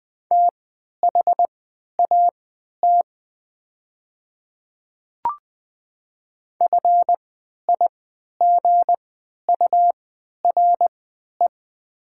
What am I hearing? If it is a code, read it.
Morse code, heard as a keyed tone, about 20 wpm, spaced as for 11 wpm, 700 Hz. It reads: THAT FIGURE
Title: 3x Character Spacing - Top 500 Words - 20wpm